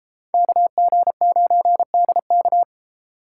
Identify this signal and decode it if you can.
Morse code, heard as a keyed tone, about 31 wpm, 700 Hz. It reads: KG9DK